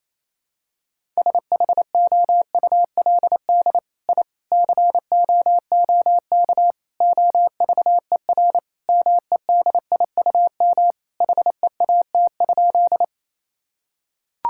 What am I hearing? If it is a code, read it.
Morse code, heard as a keyed tone, about 27 wpm, 700 Hz. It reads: SHOULD I COOK OVER MEDIUM HEAT?
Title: Sentences from Top 1000 Words - Review Part 6 - 28wpm